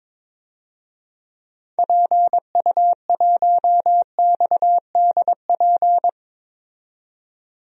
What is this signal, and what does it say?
Morse code, heard as a keyed tone, about 21 wpm, 700 Hz. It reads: PU1XDP
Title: DXCC-100 Call Signs - 6 Characters - 22wpm